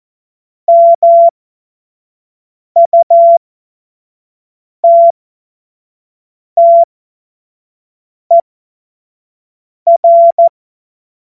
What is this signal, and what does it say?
Morse code, heard as a keyed tone, about 14 wpm, spaced as for 5 wpm, 680 Hz. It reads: MUTTER
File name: Morsekurs / Witze / Witz 14. Februar 2023 25 F70